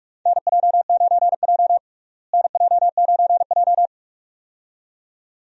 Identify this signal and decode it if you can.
Morse code, heard as a keyed tone, about 46 wpm, 700 Hz. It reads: NJ9J NJ9J
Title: USA Call Signs - 4 Characters - Repeated 2x - 45wpm